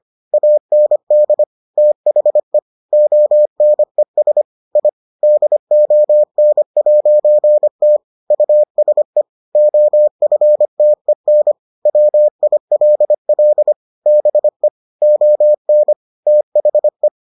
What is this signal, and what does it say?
Morse code, heard as a keyed tone, about 25 wpm, 600 Hz. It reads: AND THE ONES I DON'T USE OFTEN WILL BE ON THE